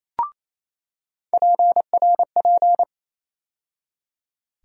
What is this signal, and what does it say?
Morse code, heard as a keyed tone, about 28 wpm, 700 Hz. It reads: PRP
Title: USA Call Sign Postfixes - 3 Letters 28wpm